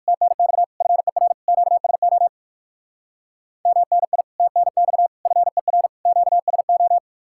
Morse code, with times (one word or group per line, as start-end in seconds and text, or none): 0.1-0.6 TNX
0.8-1.3 FER
1.5-2.3 QSO
3.7-4.2 MNI
4.4-5.1 TNX
5.3-5.9 FER
6.1-7.0 QSO